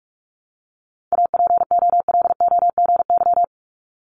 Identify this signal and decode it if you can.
Morse code, heard as a keyed tone, about 46 wpm, 700 Hz. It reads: APOLOGY
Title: Multi-letter Phonemes 45wpm